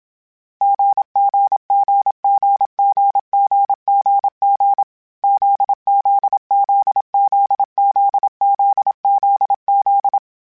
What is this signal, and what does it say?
Morse code, heard as a keyed tone, about 27 wpm, 800 Hz. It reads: GGGGGGGG ZZZZZZZZ
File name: CW Code Groups-mp3s / Sets mp3s / GZ GZ7 sp @25